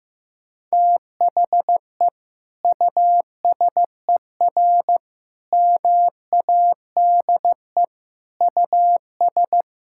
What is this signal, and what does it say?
Morse code, heard as a keyed tone, about 15 wpm, 700 Hz. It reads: THE USER MADE US
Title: Sentences from Top 1000 Words - Review Part 5 - 15wpm